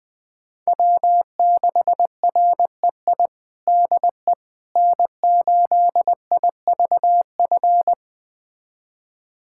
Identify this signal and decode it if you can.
Morse code, heard as a keyed tone, about 20 wpm, 700 Hz. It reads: W6REI DE N8IVF
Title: USA Callsign DE Callsign 20wpm